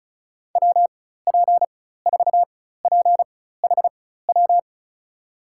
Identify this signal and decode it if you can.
Morse code, heard as a keyed tone, about 34 wpm, spaced as for 16 wpm, 700 Hz. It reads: WP4PHW
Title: USA Call Signs - 4x Character Spacing - 35wpm